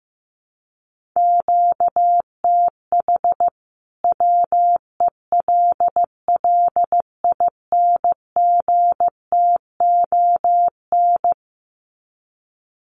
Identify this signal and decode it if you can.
Morse code, heard as a keyed tone, about 15 wpm, 700 Hz. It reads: QTH WELLINGTON
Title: Common QSO Phrases 15wpm